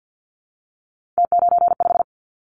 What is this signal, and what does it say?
Morse code, heard as a keyed tone, about 50 wpm, 700 Hz. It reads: T95
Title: Callsign Prefixes for Top 50 DXCC - 50wpm